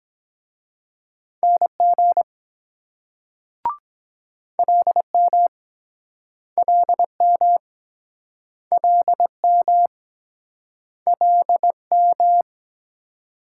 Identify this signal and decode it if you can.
Morse code, heard as a keyed tone, about 24 wpm, 700 Hz. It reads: NG LM LM LM LM